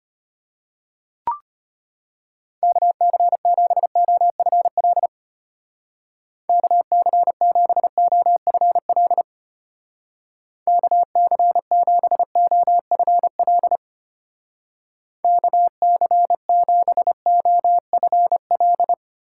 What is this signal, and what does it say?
Morse code, heard as a keyed tone, about 29 wpm, 700 Hz. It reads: KC7OFL KC7OFL KC7OFL KC7OFL
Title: Speed Racing - USA 2x3 Call Signs 25wpm